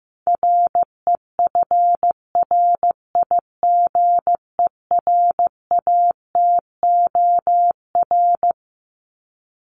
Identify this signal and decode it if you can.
Morse code, heard as a keyed tone, about 15 wpm, 700 Hz. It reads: REFRIGERATOR
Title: Top 1501-1750 Words 15wpm